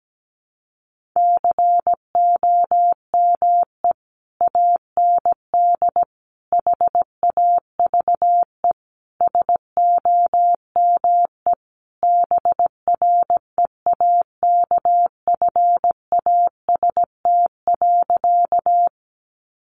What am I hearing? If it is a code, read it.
Morse code, heard as a keyed tone, about 17 wpm, 700 Hz. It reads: COME AND HAVE SOME BREAKFAST.
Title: Sentences from Top 1500 Words 17wpm